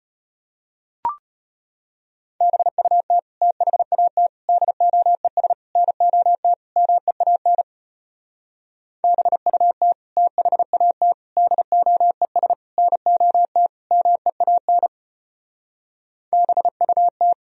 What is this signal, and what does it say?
Morse code, heard as a keyed tone, about 33 wpm, 700 Hz. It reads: BUT THAT DOES NOT MEAN BUT THAT DOES NOT MEAN BUT